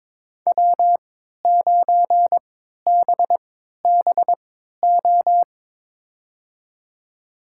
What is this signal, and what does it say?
Morse code, heard as a keyed tone, about 22 wpm, spaced as for 12 wpm, 700 Hz. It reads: W9BBO
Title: USA Call Signs - 5 Characters - 3x Character Spacing - 22wpm